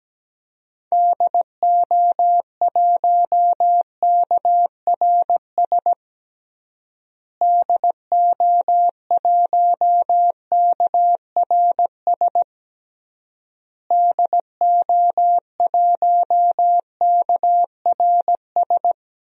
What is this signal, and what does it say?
Morse code, heard as a keyed tone, about 17 wpm, 700 Hz. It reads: DO1KRS DO1KRS DO1KRS
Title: DXCC-100 Call Signs - 6 Characters - Repeated 3x - 17wpm